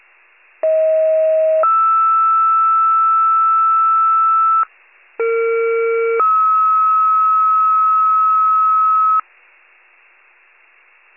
Two-tone page, heard as a one-second tone, then a three-second tone, about 640 then 1320 Hz, about 460 then 1290 Hz. It is heard from 0.6 to 9.2 s.
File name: pager / 21 RESCUE / wcalert